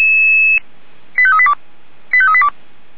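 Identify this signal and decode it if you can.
Five-tone selective call (ZVEI), heard as quick runs of five tones, tones of about 70 ms. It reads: 85382 85382
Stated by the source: Five-tone ZVEI selective call received by SDR.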